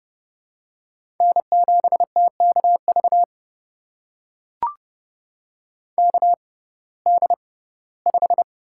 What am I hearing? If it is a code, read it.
Morse code, heard as a keyed tone, about 30 wpm, spaced as for 18 wpm, 700 Hz. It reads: N7TKV KD5
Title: USA Call Signs - 5 Characters - 6x Character Spacing - 30wpm